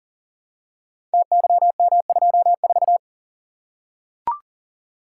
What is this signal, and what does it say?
Morse code, heard as a keyed tone, about 40 wpm, 690 Hz. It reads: TYM24
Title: Code Groups - 5 Letter:Numbers - 40wpm